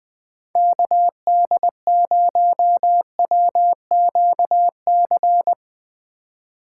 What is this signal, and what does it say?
Morse code, heard as a keyed tone, about 20 wpm, 700 Hz. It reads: KD0WQC
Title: Speed Racing - USA 2x3 Call Signs 20wpm